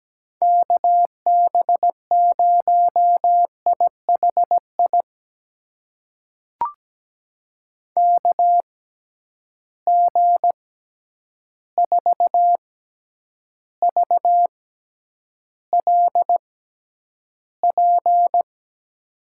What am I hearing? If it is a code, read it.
Morse code, heard as a keyed tone, about 17 wpm, spaced as for 9 wpm, 700 Hz. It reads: KB0IHI KG4VLP
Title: USA Call Signs - 6 Characters - 6x Character Spacing - 17wpm